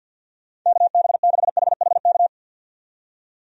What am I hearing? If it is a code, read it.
Morse code, heard as a keyed tone, about 51 wpm, 700 Hz. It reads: KB6HHK
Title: USA Call Signs - 6 Characters - 50wpm